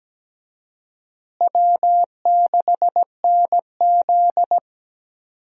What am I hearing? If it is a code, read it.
Morse code, heard as a keyed tone, about 17 wpm, 700 Hz. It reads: W6NZ
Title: USA Call Signs - 4 Characters - Repeated 2x - 17wpm